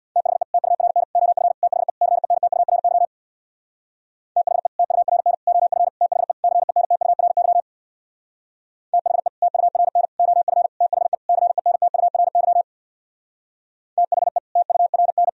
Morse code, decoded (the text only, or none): THE TURN OF THE CENTURY THE TURN OF THE CENTURY THE TURN OF THE CENTURY THE TURN